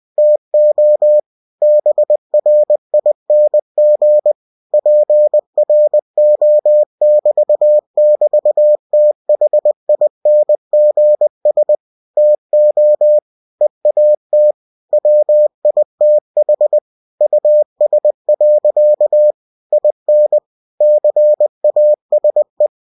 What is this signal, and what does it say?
Morse code, heard as a keyed tone, about 20 wpm, 600 Hz. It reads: TO BRING PRO<BT><BT>THINGS TO EAT WITH US. IN CASE
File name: cw-books / winnie-the-pooh / winnie-20wpm-page-0078